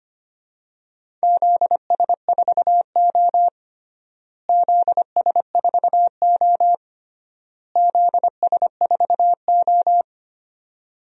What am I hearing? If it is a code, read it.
Morse code, heard as a keyed tone, about 25 wpm, 700 Hz. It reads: ZS4O ZS4O ZS4O
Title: DXCC-100 Call Signs - 4 Characters - Repeated 3x - 25wpm